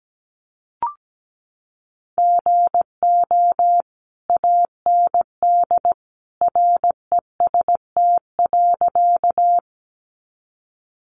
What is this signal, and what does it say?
Morse code, heard as a keyed tone, about 17 wpm, 700 Hz. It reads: GO AND REST.
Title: Sentences from Top 500 Words 17wpm